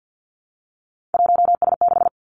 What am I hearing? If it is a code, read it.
Morse code, heard as a keyed tone, about 50 wpm, 700 Hz. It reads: 1S6
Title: Three Challenging Letter-Number Combinations 50wpm